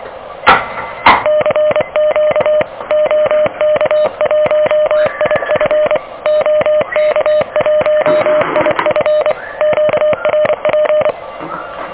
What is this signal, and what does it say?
Morse code, heard as a keyed tone, about 24 wpm, 600 Hz, with noise. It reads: CQ OK1IF OK1IF QRP
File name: OK QRP 2023 / OK QRP 2023